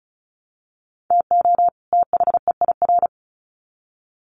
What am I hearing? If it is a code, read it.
Morse code, heard as a keyed tone, about 34 wpm, 700 Hz. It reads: TO THEIR
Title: Sets of 2 Words 35wpm